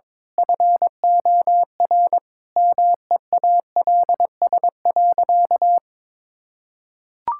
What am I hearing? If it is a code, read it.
Morse code, heard as a keyed tone, about 21 wpm, 700 Hz. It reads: FOR MEALS.
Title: Pangram Challenge v2 - 22wpm